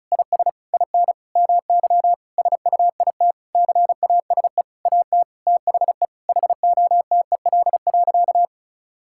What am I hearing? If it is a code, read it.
Morse code, heard as a keyed tone, about 36 wpm, 700 Hz. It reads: IS IN MY SUITCASE AT THE HOTEL.
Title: Sentences from Top 1600 Words 35wpm